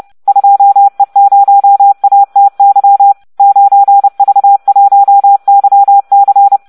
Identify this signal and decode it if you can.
Morse code, heard as a keyed tone, about 30 wpm, 800 Hz. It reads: 2E0ATY 9V1YC